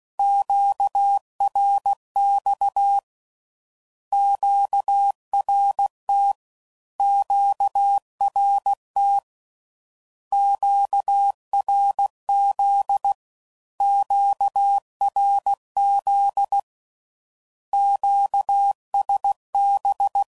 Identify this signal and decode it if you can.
Morse code, heard as a keyed tone, about 16 wpm, 800 Hz. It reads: QRX QRT QRT QRZ QRZ QSB
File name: CW Code Groups @15WPM / Q Calls / Q Calls Twice